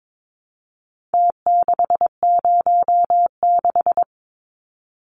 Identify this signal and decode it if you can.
Morse code, heard as a keyed tone, about 22 wpm, 700 Hz. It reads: T606
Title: Numbers - Contest Style - 4-Digit 22wpm